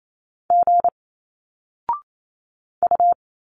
Morse code, heard as a keyed tone, about 28 wpm, 700 Hz. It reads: G U
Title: Single Letters 28wpm